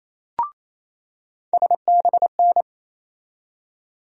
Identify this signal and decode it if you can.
Morse code, heard as a keyed tone, about 27 wpm, 700 Hz. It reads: SBN
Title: USA Call Sign Postfixes - 3 Letters 28wpm